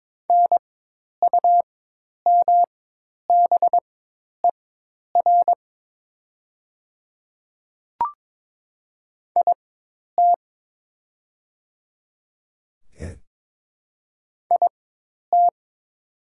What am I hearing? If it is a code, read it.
Morse code, heard as a keyed tone, about 22 wpm, spaced as for 10 wpm, 700 Hz. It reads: NUMBER IT IT
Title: 4x Character Spacing - Top 100 Words - 22wpm